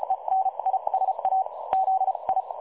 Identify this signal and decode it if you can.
Morse code, heard as a keyed tone, about 36 wpm, spaced as for 27 wpm, 760 Hz, with radio noise. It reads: ENISAZI
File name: dlazaawansowanych / nagraniazpasma / websdr recording 3565.0kHz